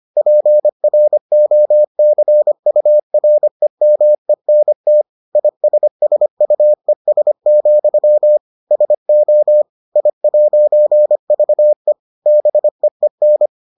Morse code, heard as a keyed tone, about 25 wpm, 600 Hz. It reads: PROCUREMENT ISSUES, SO I'VE BEEN